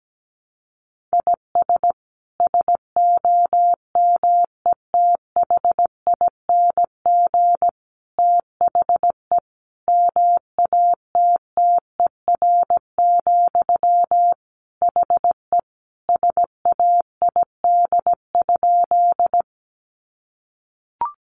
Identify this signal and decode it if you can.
Morse code, heard as a keyed tone, about 17 wpm, 700 Hz. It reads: IS SOMETHING THE MATTER, HE SAID?
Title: Sentences from Top 700 Words 17wpm